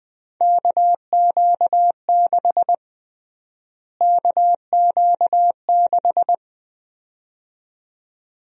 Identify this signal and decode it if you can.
Morse code, heard as a keyed tone, about 21 wpm, 700 Hz. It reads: KQ6 KQ6
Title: USA Call Sign Prefixes - Repeated 2x - 20wpm